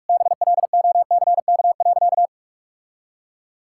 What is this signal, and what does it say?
Morse code, heard as a keyed tone, about 47 wpm, 700 Hz. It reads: BLOCK.
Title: Sentences from Top 1600 Words 45wpm